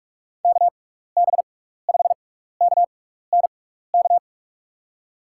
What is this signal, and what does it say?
Morse code, heard as a keyed tone, about 44 wpm, spaced as for 15 wpm, 700 Hz. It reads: KB5KNK
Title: USA Call Signs - 6 Characters - 6x Character Spacing - 45wpm